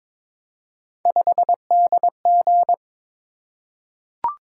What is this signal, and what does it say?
Morse code, heard as a keyed tone, about 23 wpm, 700 Hz. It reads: 5DG
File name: practice / 22wpm / Three Letter-Number Combinations - 22wpm